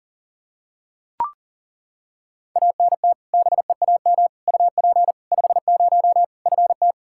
Morse code, heard as a keyed tone, about 37 wpm, 700 Hz. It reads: ANT BEAM UP 50 FT
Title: Practice QSO - Exchange 2 - Info - 40wpm